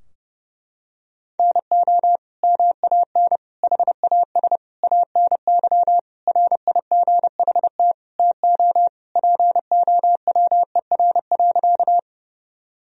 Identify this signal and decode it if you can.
Morse code, heard as a keyed tone, about 32 wpm, 700 Hz. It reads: NO MAN HAS ANY RIGHT TO POWER.
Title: Sentences from Top 1000 Words - Review Part 2 - 30wpm